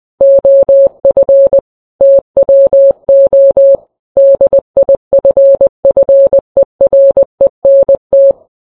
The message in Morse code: OF TWO DIFFERENT